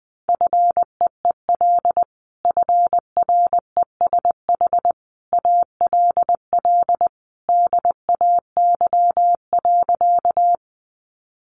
Morse code, heard as a keyed tone, about 20 wpm, 700 Hz. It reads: FEEL FRESH ALL DAY.
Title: Sentences from Top 1200 Words 20wpm